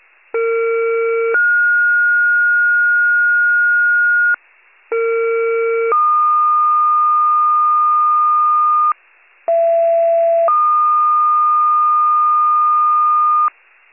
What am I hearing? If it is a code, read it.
Two-tone page, heard as a one-second tone, then a three-second tone, about 460 then 1430 Hz, about 460 then 1220 Hz, about 670 then 1220 Hz. It is heard from 0.3 to 13.5 s.